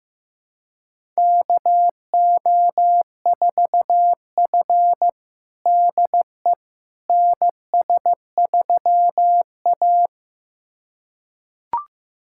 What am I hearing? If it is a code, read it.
Morse code, heard as a keyed tone, about 15 wpm, 700 Hz. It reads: KO4F DE NS3A
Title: USA Callsign DE Callsign - 4 Characters - 15wpm